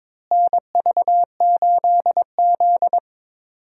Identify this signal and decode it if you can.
Morse code, heard as a keyed tone, about 21 wpm, 700 Hz. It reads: NV8Z
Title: USA Call Signs - 4 Characters - Repeated 3x - 22wpm